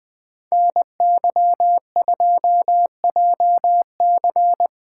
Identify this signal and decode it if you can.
Morse code, heard as a keyed tone, about 21 wpm, 700 Hz. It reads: NY2JC